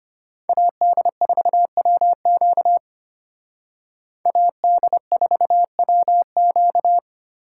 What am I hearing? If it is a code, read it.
Morse code, heard as a keyed tone, about 32 wpm, 700 Hz. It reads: AD4WQ AD4WQ